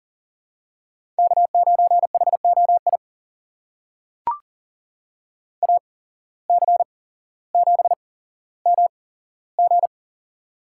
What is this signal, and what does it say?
Morse code, heard as a keyed tone, about 41 wpm, spaced as for 17 wpm, 700 Hz. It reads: K9HOI AC7MG
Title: USA Call Signs - 5 Characters - 8x Character Spacing - 40wpm